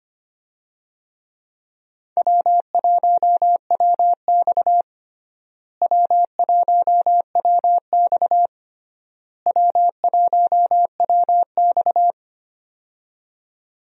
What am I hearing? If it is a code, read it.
Morse code, heard as a keyed tone, about 25 wpm, 700 Hz. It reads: W1WX W1WX W1WX
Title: USA Call Signs - 4 Characters - Repeated 3x - 25wpm